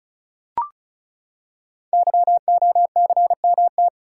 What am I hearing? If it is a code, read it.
Morse code, heard as a keyed tone, about 36 wpm, 700 Hz. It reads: YOCMT
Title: Code Groups - 5 Letters - 35wpm